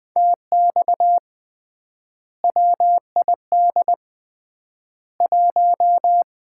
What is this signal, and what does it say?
Morse code, heard as a keyed tone, about 20 wpm, 700 Hz. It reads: TX WID 1
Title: Common QSO Phrases - 3x Word-Spacing - 20wpm